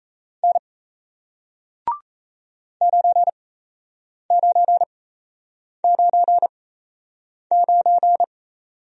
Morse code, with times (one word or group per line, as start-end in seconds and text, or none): 0.4-0.6 N
2.8-8.2 9999